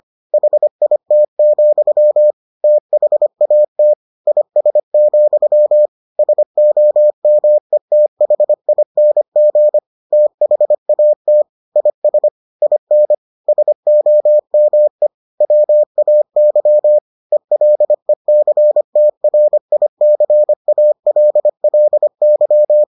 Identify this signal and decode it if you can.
Morse code, heard as a keyed tone, about 25 wpm, 600 Hz. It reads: HIT, THAT IS, SOMETHING THAT IS IN SOME WAY ELECTRICALLY